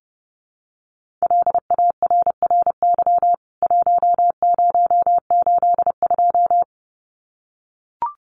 Morse code, heard as a keyed tone, about 30 wpm, 700 Hz. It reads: LARRY 1082
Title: CWT Exchange 30wpm